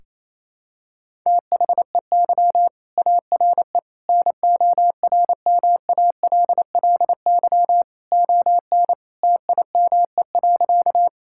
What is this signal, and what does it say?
Morse code, heard as a keyed tone, about 28 wpm, 700 Hz. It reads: THEY ARE NORMALLY ON TIME.
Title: Sentences from Top 1000 Words - Review Part 6 - 28wpm